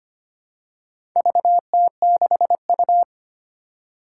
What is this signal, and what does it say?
Morse code, heard as a keyed tone, about 25 wpm, 700 Hz. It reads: VT6U
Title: Four Challenging Letter-Number Combinations 25wpm